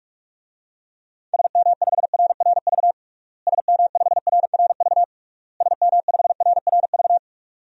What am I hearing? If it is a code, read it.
Morse code, heard as a keyed tone, about 41 wpm, 700 Hz. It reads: SM5RRV SM5RRV SM5RRV
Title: DXCC-100 Call Signs - 6 Characters - Repeated 3x - 45wpm